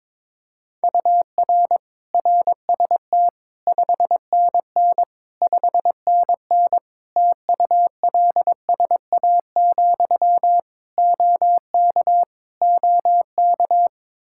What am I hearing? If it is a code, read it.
Morse code, heard as a keyed tone, about 22 wpm, 700 Hz. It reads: UR RST 5NN 5NN TULSA, OK OK
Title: Practice QSO - Exchange 1 - Beginning - 22wpm